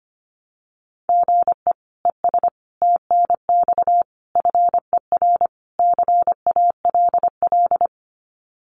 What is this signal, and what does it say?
Morse code, heard as a keyed tone, about 25 wpm, 700 Hz. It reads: GE ES TNX FER CALL
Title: Common QSO Phrases Encore 25wpm